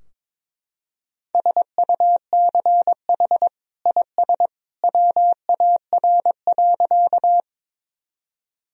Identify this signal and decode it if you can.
Morse code, heard as a keyed tone, about 22 wpm, 700 Hz. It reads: SUCH IS WAR.
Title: Sentences from Top 1000 Words - Review Part 2 - 22wpm